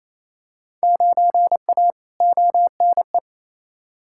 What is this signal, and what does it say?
Morse code, heard as a keyed tone, about 28 wpm, 700 Hz. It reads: 9A ONE
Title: Field Day Exchanges Encore - 28wpm